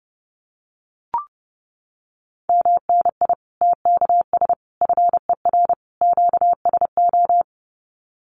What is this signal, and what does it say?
Morse code, heard as a keyed tone, about 30 wpm, 700 Hz. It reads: MNI TKS FER QSO
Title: Common QSO Phrases 30wpm